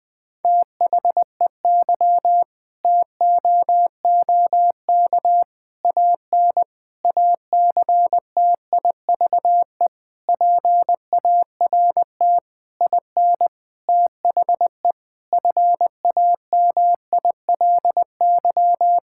Morse code, heard as a keyed tone, about 21 wpm, 700 Hz. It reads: THEY TOOK AN ACTIVE PART IN THE FAMILY
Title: Sentences from Top 1000 Words - Review Part 6 - 20wpm